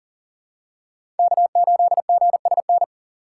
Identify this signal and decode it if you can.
Morse code, heard as a keyed tone, about 40 wpm, 700 Hz. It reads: K8GSN
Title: USA Call Signs - 3x Character Spacing - 40wpm